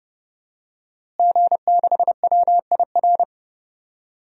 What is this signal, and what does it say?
Morse code, heard as a keyed tone, about 30 wpm, 700 Hz. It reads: G6WIR
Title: DXCC-100 Call Signs - 5 Characters - Repeated 2x - 30wpm